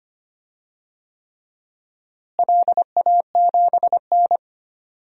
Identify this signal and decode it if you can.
Morse code, heard as a keyed tone, about 25 wpm, 700 Hz. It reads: LA7N
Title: DXCC-100 Call Signs - 4 Characters - Repeated 3x - 25wpm